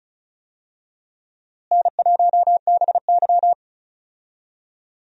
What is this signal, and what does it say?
Morse code, heard as a keyed tone, about 36 wpm, 700 Hz. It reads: N1BY
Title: USA Call Signs - 4 Characters - Repeated 2x - 35wpm